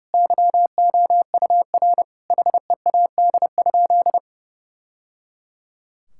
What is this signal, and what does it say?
Morse code, heard as a keyed tone, about 30 wpm, 700 Hz. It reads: YOUR HEAD?
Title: Sentences from Top 1000 Words - Review Part 5 - 30wpm